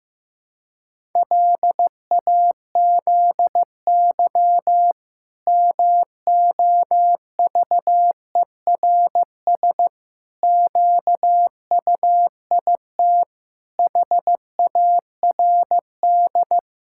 Morse code, heard as a keyed tone, about 15 wpm, 700 Hz. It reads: LAZY MOVERS QUIT HARD
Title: Pangram Challenge v2 - 15wpm